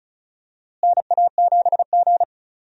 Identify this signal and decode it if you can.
Morse code, heard as a keyed tone, about 34 wpm, 700 Hz. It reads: NA7G
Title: USA Call Signs - 4 Characters - Repeated 3x - 35wpm